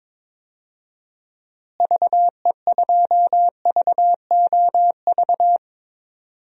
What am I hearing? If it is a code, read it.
Morse code, heard as a keyed tone, about 22 wpm, 700 Hz. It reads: VE2VOV
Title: DXCC-100 Call Signs - 6 Characters - Repeated 3x - 22wpm